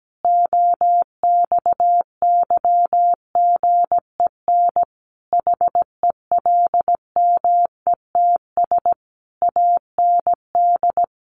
Morse code, heard as a keyed tone, about 17 wpm, 700 Hz. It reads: OXYGEN HELMETS AND
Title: A Sound Of Thunder 17wpm